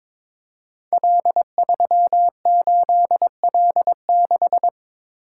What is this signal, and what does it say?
Morse code, heard as a keyed tone, about 23 wpm, 700 Hz. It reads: L38L6
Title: Code Groups - 5 Letter:Numbers - 22wpm